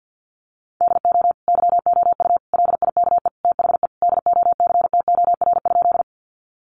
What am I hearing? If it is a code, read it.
Morse code, heard as a keyed tone, about 49 wpm, 700 Hz. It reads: DO YOU LIKE THE DOCTOR?
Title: Sentences from Top 1500 Words 50wpm